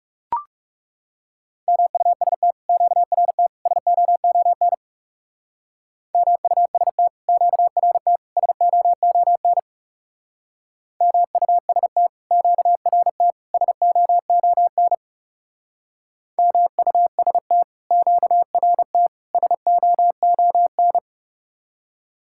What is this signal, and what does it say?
Morse code, heard as a keyed tone, about 36 wpm, 700 Hz. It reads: MUST QRT SOON MUST QRT SOON MUST QRT SOON MUST QRT SOON
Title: Speed-Racing Common QSO Phrases - 30wpm